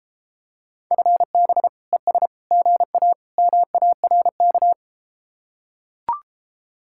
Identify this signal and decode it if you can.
Morse code, heard as a keyed tone, about 34 wpm, 700 Hz. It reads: FB ES GA MARK